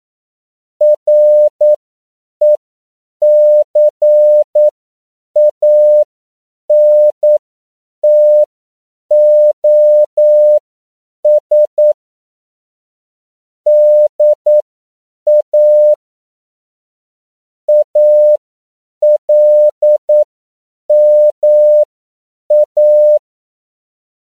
Morse code, heard as a keyed tone, about 9 wpm, spaced as for 7 wpm, 600 Hz. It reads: RECANTOS DA ALMA
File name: CW / 10PPM